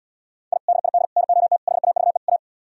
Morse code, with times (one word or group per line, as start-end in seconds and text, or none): 0.5-0.6 I
0.7-1.1 DID
1.2-1.6 NOT
1.7-2.2 HAVE
2.3-2.4 A